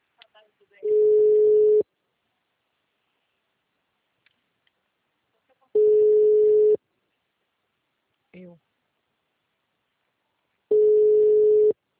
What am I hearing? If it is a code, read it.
Ringback tone (Europe, 425 Hz), with speech.
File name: out- -2024- - - .109